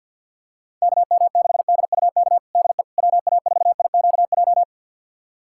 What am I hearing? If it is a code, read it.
Morse code, heard as a keyed tone, about 51 wpm, 690 Hz. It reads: KM6DUK DE WA4IQJ
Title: USA Callsign DE Callsign - 6 Characters - 50wpm